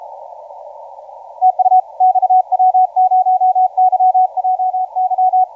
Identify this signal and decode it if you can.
Morse code, heard as a keyed tone, about 33 wpm, 740 Hz, with radio noise. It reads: TU XW0YJY
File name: XW0YJY 1813UTC 12m